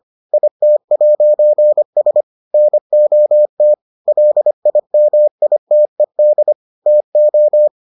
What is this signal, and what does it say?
Morse code, heard as a keyed tone, about 25 wpm, 600 Hz. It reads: IT'S NOT LIMITED TO